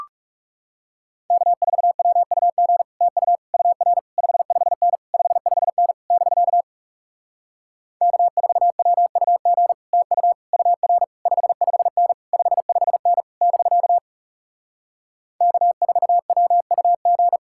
Morse code, heard as a keyed tone, about 40 wpm, 700 Hz. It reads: K4WUG TU UR 55N 55N <BK> K4WUG TU UR 55N 55N <BK> K4WUG